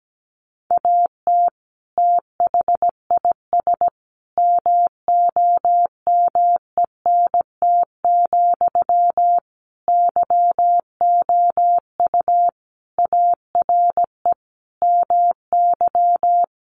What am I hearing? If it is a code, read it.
Morse code, heard as a keyed tone, about 17 wpm, 700 Hz. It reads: AT THIS MOMENT, YOU ARE MY
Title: Sentences from Top 1000 Words 17wpm